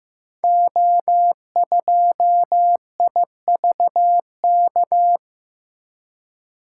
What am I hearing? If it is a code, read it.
Morse code, heard as a keyed tone, about 15 wpm, 700 Hz. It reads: O2IVK
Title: Code Groups - 5 Letter:Numbers - 15wpm